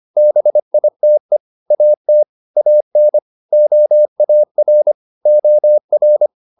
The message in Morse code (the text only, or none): BITE AT AN OAR OR